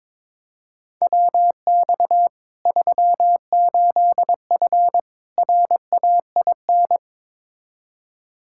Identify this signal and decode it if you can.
Morse code, heard as a keyed tone, about 22 wpm, 700 Hz. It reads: WX 38F RAIN
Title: Practice QSO - Exchange 2 - Info - 22wpm